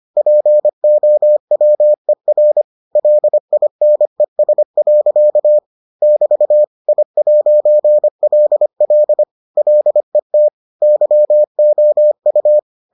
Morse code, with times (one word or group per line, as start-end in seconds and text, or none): 0.2-2.6 POWER
3.0-5.6 LINES.
6.0-6.6 <BT>
6.9-9.2 I'LL
9.6-10.5 LET
10.8-12.6 YOU